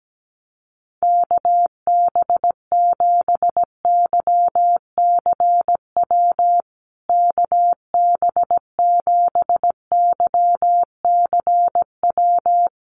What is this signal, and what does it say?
Morse code, heard as a keyed tone, about 17 wpm, 700 Hz. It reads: KB7YCW KB7YCW
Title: USA Call Signs Repeated 2x 17wpm